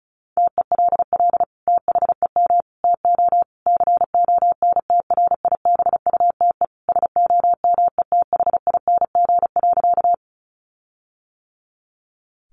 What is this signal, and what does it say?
Morse code, heard as a keyed tone, about 34 wpm, 700 Hz. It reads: TELL THEM TO CONTRIBUTE SOMETHING.
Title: Sentences from Top 1400 Words 35wpm